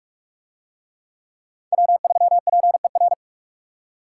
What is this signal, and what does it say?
Morse code, heard as a keyed tone, about 45 wpm, 700 Hz. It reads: W3PER